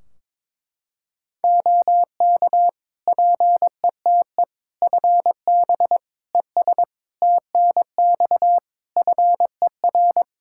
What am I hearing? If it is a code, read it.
Morse code, heard as a keyed tone, about 23 wpm, 700 Hz. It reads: OK PETE FB ES TNX FER
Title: Practice QSO - Exchange 2 - Info - 22wpm